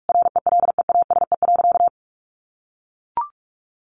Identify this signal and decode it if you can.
Morse code, heard as a keyed tone, about 44 wpm, 700 Hz. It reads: RELEASE.
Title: Sentences from Top 1100 Words 45wpm